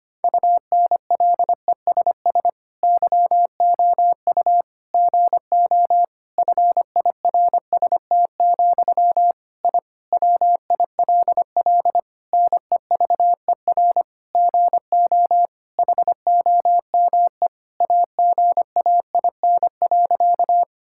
Morse code, heard as a keyed tone, about 25 wpm, 700 Hz. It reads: UNLESS YOU GO FIRST, I WILL NEVER GO HOME AGAIN.
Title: Sentences from Top 1000 Words - Review Part 6 - 25wpm